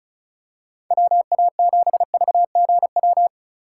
Morse code, heard as a keyed tone, about 36 wpm, 700 Hz. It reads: WA7VGW
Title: USA Call Signs - 6 Characters - 3x Character Spacing - 35wpm